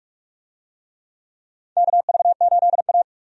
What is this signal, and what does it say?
Morse code, heard as a keyed tone, about 46 wpm, 700 Hz. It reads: KV8A
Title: USA Call Signs - 4 Characters - 45wpm